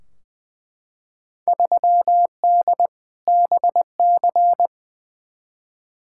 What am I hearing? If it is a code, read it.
Morse code, heard as a keyed tone, about 20 wpm, 700 Hz. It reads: 3D BC
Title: Field Day Exchanges Encore - 20wpm